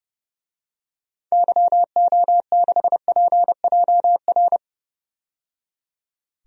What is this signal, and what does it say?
Morse code, heard as a keyed tone, about 30 wpm, 700 Hz. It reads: YO6PJR
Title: DXCC-100 Call Signs - 6 Characters - 30wpm